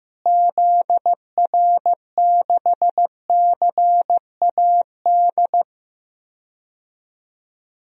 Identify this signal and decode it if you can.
Morse code, heard as a keyed tone, about 16 wpm, 700 Hz. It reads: ZR6CAD
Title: DXCC-100 Call Signs - 6 Characters - Repeated 2x - 15wpm